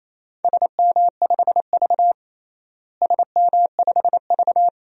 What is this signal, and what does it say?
Morse code, heard as a keyed tone, about 28 wpm, 700 Hz. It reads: SM5V SM5V
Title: DXCC-100 Call Signs - 4 Characters - Repeated 2x - 28wpm